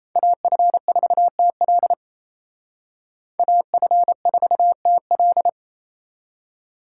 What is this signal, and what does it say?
Morse code, heard as a keyed tone, about 33 wpm, 700 Hz. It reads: AF4TL AF4TL
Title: Speed Racing - USA 2x2 Call Signs 28wpm